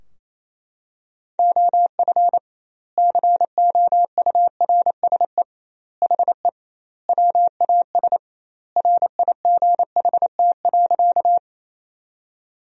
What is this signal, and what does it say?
Morse code, heard as a keyed tone, about 28 wpm, 700 Hz. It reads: OF COURSE HE WAS RIGHT.
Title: 2x Word-Spacing - Sentences from Top 500 Words - 28wpm